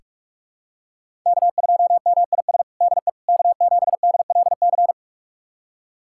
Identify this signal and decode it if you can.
Morse code, heard as a keyed tone, about 47 wpm, 700 Hz. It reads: K2MIS DE K7DLC
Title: USA Callsign DE Callsign - 5 Characters - 45wpm